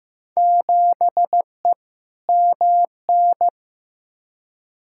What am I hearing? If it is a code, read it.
Morse code, heard as a keyed tone, about 15 wpm, 700 Hz. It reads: 7E MN
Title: Field Day Exchanges Encore - 15wpm